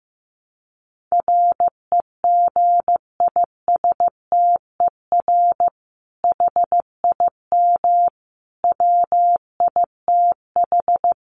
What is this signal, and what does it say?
Morse code, heard as a keyed tone, about 15 wpm, 700 Hz. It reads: REGISTER HIM WITH
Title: Sentences from Top 1200 Words 15wpm